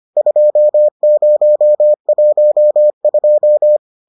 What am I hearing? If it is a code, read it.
Morse code, heard as a keyed tone, about 25 wpm, 600 Hz. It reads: 2012